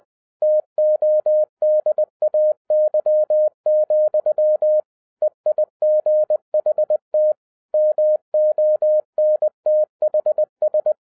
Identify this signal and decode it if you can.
Morse code, heard as a keyed tone, about 20 wpm, 610 Hz. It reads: TODAY, EIGHT MONTHS
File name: 2012-01-mp3 / n16-THROUGH EIGHT MONTHS-20wpm